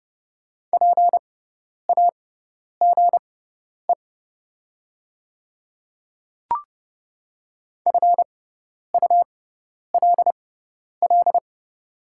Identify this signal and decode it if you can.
Morse code, heard as a keyed tone, about 30 wpm, spaced as for 10 wpm, 700 Hz. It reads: PAGE FULL
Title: 6x Character Spacing - Top 500 Words - 30wpm